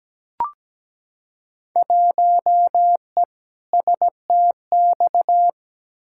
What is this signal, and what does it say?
Morse code, heard as a keyed tone, about 17 wpm, 700 Hz. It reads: 1E STX
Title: Field Day Exchanges - 17wpm